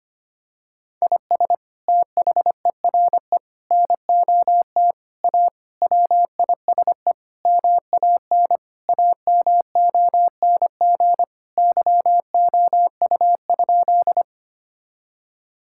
Morse code, heard as a keyed tone, about 25 wpm, 700 Hz. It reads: IS THERE NOT A WISE MAN AMONG YOU?
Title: Sentences from Top 1600 Words 25wpm